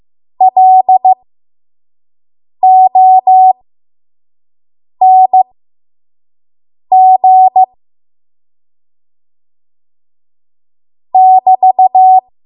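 Morse code, heard as a keyed tone, about 15 wpm, spaced as for 5 wpm, 750 Hz. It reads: LONG <BT>
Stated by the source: 5 WPM Morse code practice.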